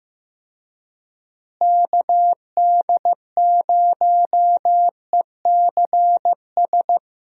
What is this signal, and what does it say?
Morse code, heard as a keyed tone, about 15 wpm, 700 Hz. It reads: KD0ECS
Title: USA Call Signs - 6 Characters - Repeated 3x - 15wpm